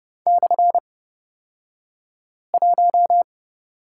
Morse code, heard as a keyed tone, about 30 wpm, spaced as for 5 wpm, 700 Hz. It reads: /1
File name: Lesson 37d - Letter x - Two Characters 30wpm